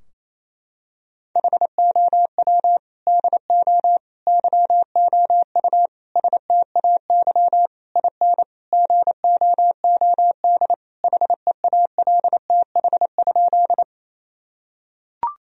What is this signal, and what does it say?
Morse code, heard as a keyed tone, about 28 wpm, 700 Hz. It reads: HOW DO YOU STAY IN GOOD HEALTH?
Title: Sentences from Top 1000 Words - Review Part 3 - 28wpm